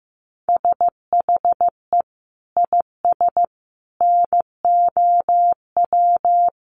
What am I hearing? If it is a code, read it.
Morse code, heard as a keyed tone, about 15 wpm, 700 Hz. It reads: SHE IS NOW
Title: Sentences from Top 1100 Words 15wpm